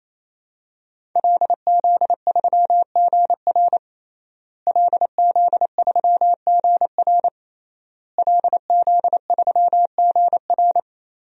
Morse code, heard as a keyed tone, about 27 wpm, 700 Hz. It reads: LZ3GR LZ3GR LZ3GR
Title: DXCC-100 Call Signs - 5 Characters - Repeated 3x - 28wpm